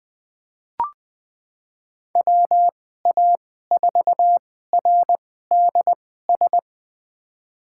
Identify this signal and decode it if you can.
Morse code, heard as a keyed tone, about 20 wpm, spaced as for 15 wpm, 700 Hz. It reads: WA4RDS
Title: USA Call Signs - 6 Characters - 2x Character Spacing - 20wpm